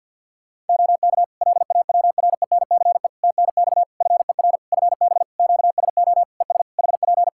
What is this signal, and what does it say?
Morse code, heard as a keyed tone, about 52 wpm, 700 Hz. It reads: OK LAWRENCE TNX FER FB QSO ES HP
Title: Practice QSO - Exchange 4 - Ending - 50wpm